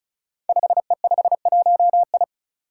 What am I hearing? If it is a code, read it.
Morse code, heard as a keyed tone, about 36 wpm, 700 Hz. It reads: 5E51I
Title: Five Challenging Letter-Number Combinations 35wpm